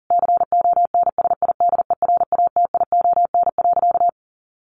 Morse code, heard as a keyed tone, about 40 wpm, 700 Hz. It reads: CONSIDERATION.
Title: Sentences from Top 1500 Words 40wpm